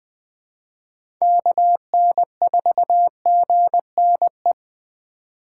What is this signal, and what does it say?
Morse code, heard as a keyed tone, about 20 wpm, 700 Hz. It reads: KN4GNE
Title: USA Call Signs - 6 Characters - 2x Character Spacing - 20wpm